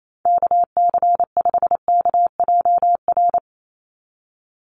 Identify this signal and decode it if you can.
Morse code, heard as a keyed tone, about 28 wpm, 700 Hz. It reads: KC5KJR
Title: USA Call Signs 28wpm